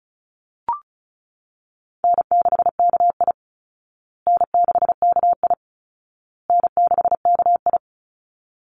Morse code, heard as a keyed tone, about 34 wpm, 700 Hz. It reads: N6KI N6KI N6KI
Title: Easy Call Signs Repeated 3x - 35wpm